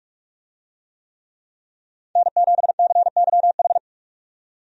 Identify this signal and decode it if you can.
Morse code, heard as a keyed tone, about 43 wpm, 700 Hz. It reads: N7CYH
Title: USA Call Signs - 5 Characters - 45wpm